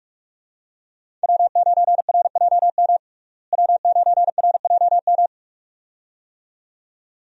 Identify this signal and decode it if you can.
Morse code, heard as a keyed tone, about 46 wpm, 700 Hz. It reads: W9RJM W9RJM
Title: USA Call Signs - 5 Characters - Repeated 2x - 45wpm